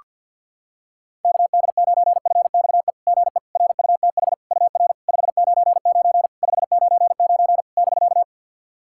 Morse code, heard as a keyed tone, about 50 wpm, 690 Hz. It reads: KD9FXE GE RUTH UR 599 599 <BK>
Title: SOTA QSO Practice 50wpm